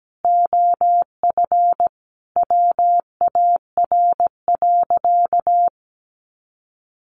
Morse code, heard as a keyed tone, about 17 wpm, 700 Hz. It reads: OF WAR.
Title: Sentences from Top 1300 Words 17wpm